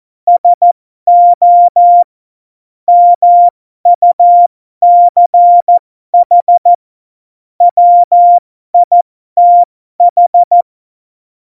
Morse code, heard as a keyed tone, about 14 wpm, spaced as for 12 wpm, 700 Hz. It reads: SO MUCH WITH
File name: Morsekurs / Witze